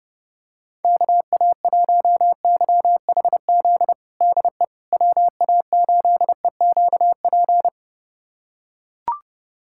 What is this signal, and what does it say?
Morse code, heard as a keyed tone, about 30 wpm, 700 Hz. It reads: KA1YHZ DE WA8EQP
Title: USA Callsign DE Callsign - 6 Characters - 30wpm